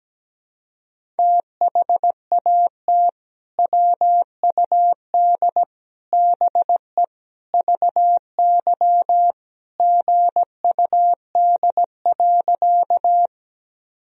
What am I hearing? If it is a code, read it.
Morse code, heard as a keyed tone, about 17 wpm, 700 Hz. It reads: THAT WUD BE VY GUD.
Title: Sentences from Top 200 Words and Abbreviations 17wpm